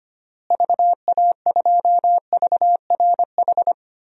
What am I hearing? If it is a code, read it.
Morse code, heard as a keyed tone, about 25 wpm, 700 Hz. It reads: VA2VRH
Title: DXCC-100 Call Signs - 6 Characters - Repeated 3x - 25wpm